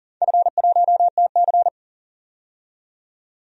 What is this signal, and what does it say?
Morse code, heard as a keyed tone, about 42 wpm, 700 Hz. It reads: F1TC
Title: DXCC-100 Call Signs - 4 Characters - 40wpm